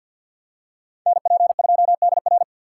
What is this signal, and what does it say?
Morse code, heard as a keyed tone, about 53 wpm, 700 Hz. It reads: NP2DR
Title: USA Call Signs - 5 Characters - 6x Character Spacing - 50wpm